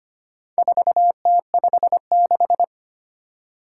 Morse code, heard as a keyed tone, about 25 wpm, 700 Hz. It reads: 4T56
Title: Four Challenging Letter-Number Combinations 25wpm